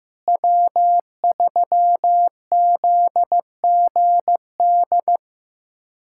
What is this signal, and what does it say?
Morse code, heard as a keyed tone, about 15 wpm, 700 Hz. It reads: W3ZGD
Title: K1USN SST QSO Practice - 15wpm